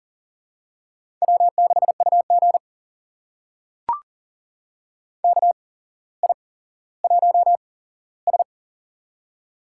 Morse code, heard as a keyed tone, about 40 wpm, spaced as for 17 wpm, 700 Hz. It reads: W6UG KI1S
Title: USA Call Signs - 4 Characters - 8x Character Spacing - 40wpm